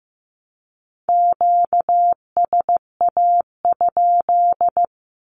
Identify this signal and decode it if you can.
Morse code, heard as a keyed tone, about 15 wpm, 700 Hz. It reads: QSA?
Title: Q-Codes 15wpm